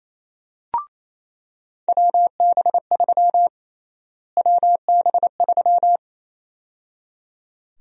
Morse code, heard as a keyed tone, about 27 wpm, 700 Hz. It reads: WB3 WB3
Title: USA Call Sign Prefixes - Repeated 2x - 28wpm